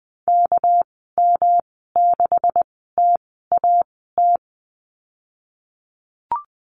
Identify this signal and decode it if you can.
Morse code, heard as a keyed tone, about 20 wpm, spaced as for 14 wpm, 700 Hz. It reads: KM6TAT
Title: 238 Lesson 38g - Letter q - Call Signs 20wpm